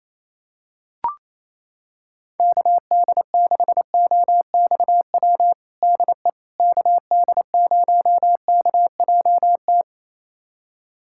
Morse code, heard as a keyed tone, about 28 wpm, 700 Hz. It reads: KD6OXW DE KD0KJT
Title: USA Callsign DE Callsign - 6 Characters - 28wpm